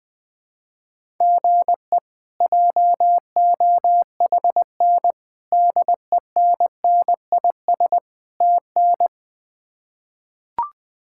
Morse code, heard as a keyed tone, about 20 wpm, 700 Hz. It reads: GE JOHN DENNIS TN
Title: K1USN SST QSO Practice - 20wpm